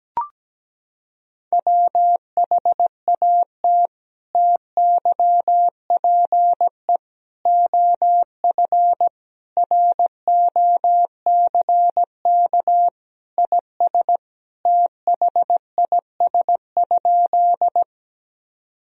Morse code, heard as a keyed tone, about 17 wpm, 700 Hz. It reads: WHAT TYPE OF ROCK IS THIS?
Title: Sentences from Top 1000 Words - Review Part 3 - 17wpm